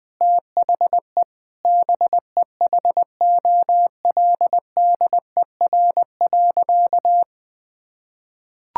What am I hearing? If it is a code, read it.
Morse code, heard as a keyed tone, about 20 wpm, 700 Hz. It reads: THE BEHOLDER.
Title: Sentences with 7-Word Phrases - 20wpm